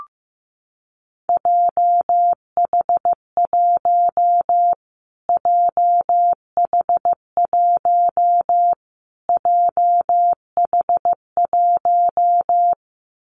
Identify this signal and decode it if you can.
Morse code, heard as a keyed tone, about 14 wpm, 700 Hz. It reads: JH1 JH1 JH1
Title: Callsign Prefixes for Top 25 DXCC - Repeated 3x - 15wpm